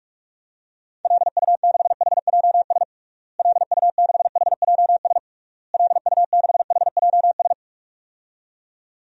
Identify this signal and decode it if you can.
Morse code, heard as a keyed tone, about 46 wpm, 700 Hz. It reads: LU6HJS LU6HJS LU6HJS
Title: DXCC-100 Call Signs - 6 Characters - Repeated 3x - 45wpm